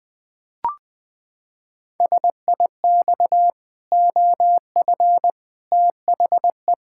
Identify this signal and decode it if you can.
Morse code, heard as a keyed tone, about 20 wpm, 700 Hz. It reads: SIX OF THE
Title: Pangram Challenge v2 - 20wpm